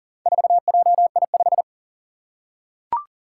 Morse code, heard as a keyed tone, about 40 wpm, 700 Hz. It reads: 4JI5
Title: Four Challenging Letter-Number Combinations 40wpm